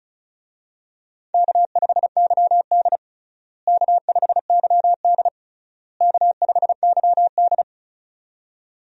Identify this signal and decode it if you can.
Morse code, heard as a keyed tone, about 34 wpm, 700 Hz. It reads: K5YD K5YD K5YD
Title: USA Call Signs - 4 Characters - Repeated 3x - 35wpm